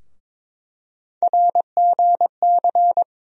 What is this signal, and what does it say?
Morse code, heard as a keyed tone, about 22 wpm, 700 Hz. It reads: RGC
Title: USA Call Sign Postfixes - 3 Letters 22wpm